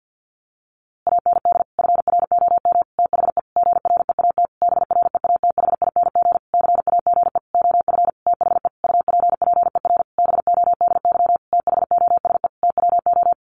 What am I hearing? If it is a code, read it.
Morse code, heard as a keyed tone, about 50 wpm, 700 Hz. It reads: AND FROM THE GREAT BREATHING CAGE OF THE UPPER BODY THOSE TWO
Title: A Sound Of Thunder 50wpm